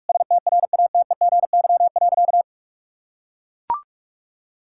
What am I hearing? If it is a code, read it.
Morse code, heard as a keyed tone, about 41 wpm, 700 Hz. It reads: STRATEGY.
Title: Sentences from Top 1000 Words - Review Part 6 - 45wpm